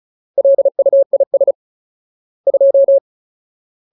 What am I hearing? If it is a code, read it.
Morse code, heard as a keyed tone, about 34 wpm, 550 Hz. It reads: LUIS 2
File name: Ultimate Word List / 35WPM Ultimate